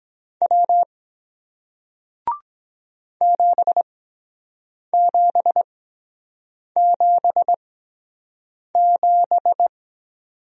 Morse code, heard as a keyed tone, about 23 wpm, spaced as for 7 wpm, 700 Hz. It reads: W 7777